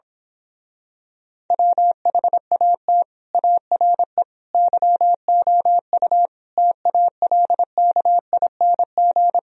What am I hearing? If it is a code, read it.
Morse code, heard as a keyed tone, about 26 wpm, 700 Hz. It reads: WHAT ARE YOU TALKING
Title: Speed-Racing - Sets of 5 Words 22wpm